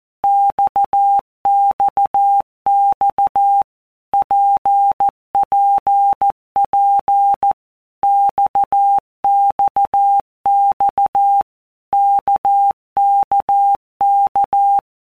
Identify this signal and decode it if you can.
Morse code, heard as a keyed tone, about 14 wpm, 800 Hz. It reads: XXX PPP XXX KKK
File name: All Groups 1-minute Blocks / Groups@13 / RKPX